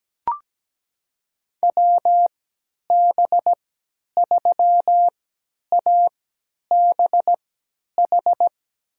Morse code, heard as a keyed tone, about 17 wpm, spaced as for 10 wpm, 700 Hz. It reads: WB3ABH
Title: USA Call Signs - 6 Characters - 3x Character Spacing - 17wpm